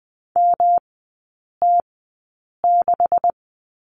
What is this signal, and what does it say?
Morse code, heard as a keyed tone, about 20 wpm, spaced as for 8 wpm, 700 Hz. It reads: MT6
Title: 218 Lesson 35e - Number 0 - Three Characters 20wpm